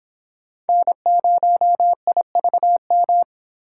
Morse code, heard as a keyed tone, about 25 wpm, 700 Hz. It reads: N0IVM